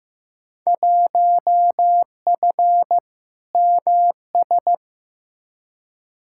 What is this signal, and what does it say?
Morse code, heard as a keyed tone, about 15 wpm, 700 Hz. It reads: 1F MS
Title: Field Day Exchanges Encore - 15wpm